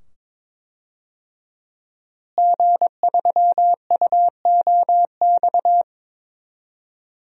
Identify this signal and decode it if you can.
Morse code, heard as a keyed tone, about 22 wpm, 700 Hz. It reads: G3UOX
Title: DXCC-100 Call Signs - 5 Characters - Repeated 2x - 22wpm